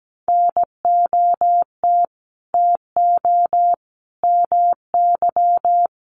Morse code, heard as a keyed tone, about 17 wpm, 700 Hz. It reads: NOT TO MY
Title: Sentences from Top 600 Words 17wpm